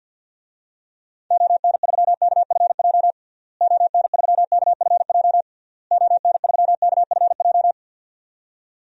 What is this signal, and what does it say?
Morse code, heard as a keyed tone, about 52 wpm, 700 Hz. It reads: ON3KFJ ON3KFJ ON3KFJ
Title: DXCC-100 Call Signs - 6 Characters - Repeated 3x - 50wpm